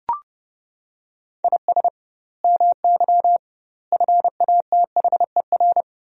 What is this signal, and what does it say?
Morse code, heard as a keyed tone, about 32 wpm, 700 Hz. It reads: IS MY FATHER
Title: 2x Word-Spacing - Sentences from Top 300 Words - 30wpm